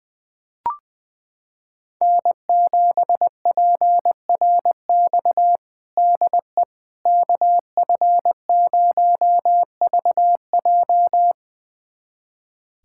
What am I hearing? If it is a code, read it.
Morse code, heard as a keyed tone, about 20 wpm, 700 Hz. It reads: N7PRX DE KF0VJ
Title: USA Callsign DE Callsign - 5 Characters - 20wpm